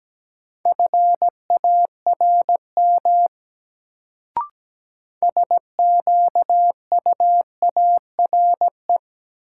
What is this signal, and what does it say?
Morse code, heard as a keyed tone, about 17 wpm, 700 Hz. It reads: FARM SQUARE
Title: Sentences from Top 1200 Words - New Word Drill Alt - 17wpm